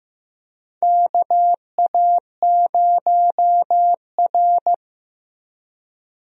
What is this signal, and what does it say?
Morse code, heard as a keyed tone, about 15 wpm, 700 Hz. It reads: KA0R
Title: USA Call Signs - 4 Characters - Repeated 3x - 15wpm